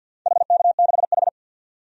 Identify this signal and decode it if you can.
Morse code, heard as a keyed tone, about 52 wpm, 700 Hz. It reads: HK6H
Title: DXCC-100 Call Signs - 4 Characters - 50wpm